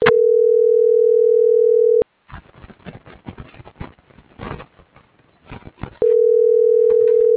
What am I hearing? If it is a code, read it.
Ringback tone (North America), 440 + 480 Hz.